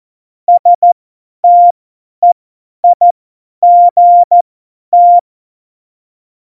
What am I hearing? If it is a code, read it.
Morse code, heard as a keyed tone, about 14 wpm, spaced as for 10 wpm, 700 Hz. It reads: STEIGT
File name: Morsekurs / Witze / Witz 27. November 2018, 50 F70